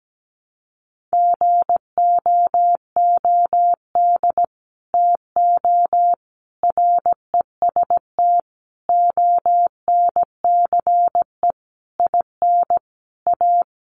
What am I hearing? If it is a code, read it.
Morse code, heard as a keyed tone, about 17 wpm, 700 Hz. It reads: GOOD TO REST ONCE IN A
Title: Sentences from Top 500 Words 17wpm